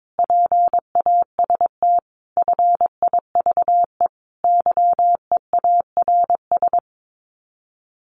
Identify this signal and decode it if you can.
Morse code, heard as a keyed tone, about 22 wpm, 700 Hz. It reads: PAST FIVE YEARS
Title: Sets of 5 Words Encore 22wpm